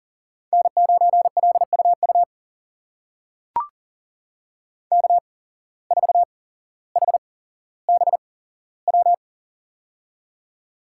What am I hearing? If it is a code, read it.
Morse code, heard as a keyed tone, about 40 wpm, spaced as for 17 wpm, 700 Hz. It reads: N9LUU K4HBW
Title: USA Call Signs - 5 Characters - 8x Character Spacing - 40wpm